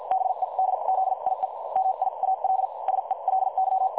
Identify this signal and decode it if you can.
Morse code, heard as a keyed tone, about 50 wpm, spaced as for 41 wpm, 760 Hz. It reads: BEDZIE MIAL SELY
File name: dlazaawansowanych / nagraniazpasma / websdr recording 3565.0kHz